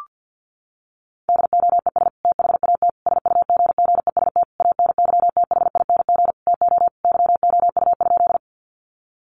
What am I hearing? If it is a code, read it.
Morse code, heard as a keyed tone, about 48 wpm, 700 Hz. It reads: DOES THAT SUGGEST ANYTHING TO YOU?
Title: Sentences from Top 1300 Words 50wpm